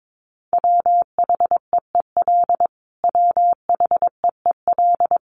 Morse code, heard as a keyed tone, about 22 wpm, 700 Hz. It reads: WHEEL WHEEL
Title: Top 500 Words Repeated 22wpm